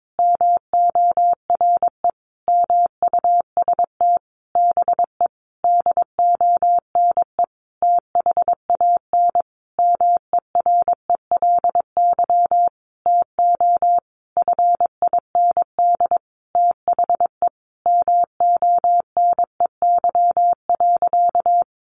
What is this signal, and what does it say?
Morse code, heard as a keyed tone, about 22 wpm, 700 Hz. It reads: MORE MUST BE DONE THAN MERELY TO FIND THE MONEY.
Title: Sentences from Top 1400 Words 22wpm